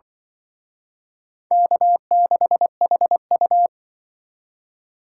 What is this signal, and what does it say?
Morse code, heard as a keyed tone, about 24 wpm, 700 Hz. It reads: K6HU
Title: Speed Racing - USA 1x2 Call Signs 20wpm